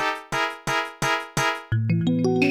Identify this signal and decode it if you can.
There is no code.